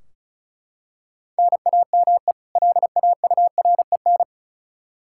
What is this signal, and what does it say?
Morse code, heard as a keyed tone, about 34 wpm, 700 Hz. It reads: NAME LAUREN
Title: K1USN SST QSO Practice 35wpm